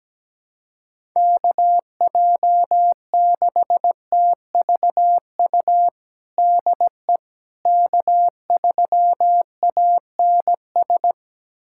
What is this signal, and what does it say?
Morse code, heard as a keyed tone, about 17 wpm, 700 Hz. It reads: KJ6TVU DE K3ANS
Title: USA Callsign DE Callsign 17wpm